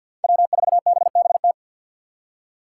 Morse code, heard as a keyed tone, about 50 wpm, 700 Hz. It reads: W4BBT
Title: K1USN SST QSO Practice - 50wpm